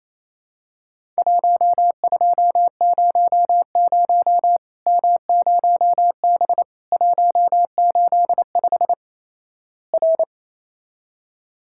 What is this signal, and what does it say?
Morse code, heard as a keyed tone, about 28 wpm, 700 Hz. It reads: 1200 M0B 185 R
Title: Worked All Europe DX Contest - QTC Exchange - 28wpm